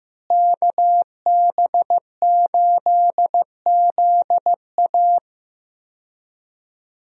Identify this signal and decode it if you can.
Morse code, heard as a keyed tone, about 16 wpm, 700 Hz. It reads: KB8ZA
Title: USA Call Signs - 5 Characters - Repeated 3x - 15wpm